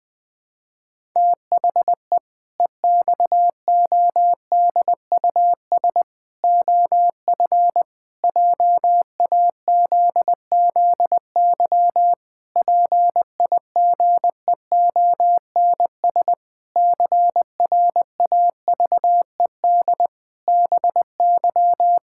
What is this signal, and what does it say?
Morse code, heard as a keyed tone, about 20 wpm, 700 Hz. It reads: THE EXODUS OF JAZZY PIGEONS CRAVED BY
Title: Pangram Challenge v2 - 20wpm